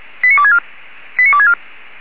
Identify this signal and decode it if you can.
Five-tone selective call (ZVEI), heard as quick runs of five tones, tones of about 70 ms. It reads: 89264 89264